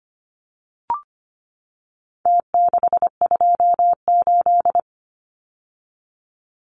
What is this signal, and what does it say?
Morse code, heard as a keyed tone, about 25 wpm, 700 Hz. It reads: T628
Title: Numbers - Contest Style - 4-Digit 25wpm